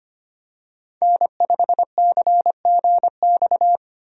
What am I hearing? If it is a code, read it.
Morse code, heard as a keyed tone, about 25 wpm, 700 Hz. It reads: N5CGX
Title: USA Call Signs - 5 Characters - Repeated 3x - 25wpm